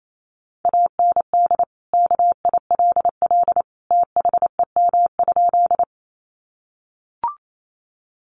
Morse code, heard as a keyed tone, about 28 wpm, 700 Hz. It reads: AND KILL THEM?
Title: Sentences from Top 1400 Words 28wpm